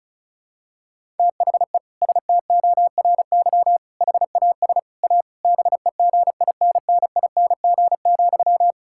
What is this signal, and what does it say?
Morse code, heard as a keyed tone, about 34 wpm, 700 Hz. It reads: THE STORY HAS A BEGINNING,
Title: Sentences from Top 1000 Words - Review Part 6 - 35wpm